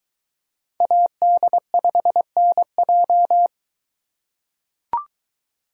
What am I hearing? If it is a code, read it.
Morse code, heard as a keyed tone, about 23 wpm, 700 Hz. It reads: AD5NJ